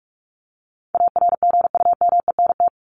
Morse code, heard as a keyed tone, about 46 wpm, 700 Hz. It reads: ARGUMENT
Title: Top 1251-1500 Words 45wpm